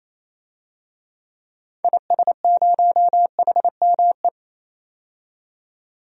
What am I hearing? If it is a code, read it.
Morse code, heard as a keyed tone, about 28 wpm, 700 Hz. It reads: IS0HME
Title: DXCC-100 Call Signs - 6 Characters - 28wpm